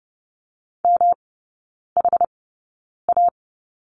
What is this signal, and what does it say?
Morse code, heard as a keyed tone, about 30 wpm, spaced as for 9 wpm, 700 Hz. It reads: MHA
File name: Lesson 18e - Letter m - Three Characters 30wpm